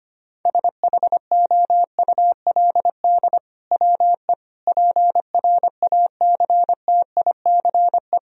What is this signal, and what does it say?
Morse code, heard as a keyed tone, about 25 wpm, 700 Hz. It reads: SHOULD WE PRACTICE
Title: Sentences from Top 1000 Words - Review Part 4 - 25wpm